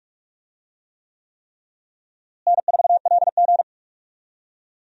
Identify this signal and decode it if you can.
Morse code, heard as a keyed tone, about 41 wpm, 700 Hz. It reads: N4LG
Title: USA Call Signs - 4 Characters - Repeated 3x - 45wpm